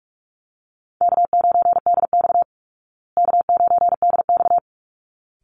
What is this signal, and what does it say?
Morse code, heard as a keyed tone, about 44 wpm, 700 Hz. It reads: K9DX K9DX
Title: Call Signs Repeated 3x - 45wpm